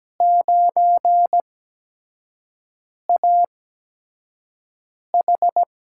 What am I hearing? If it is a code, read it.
Morse code, heard as a keyed tone, about 17 wpm, spaced as for 5 wpm, 700 Hz. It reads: 9AH
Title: USA Call Signs - 5 Characters - 8x Character Spacing - 17wpm